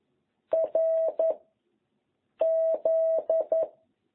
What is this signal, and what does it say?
Morse code, heard as a keyed tone, about 11 wpm, 660 Hz. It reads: R Z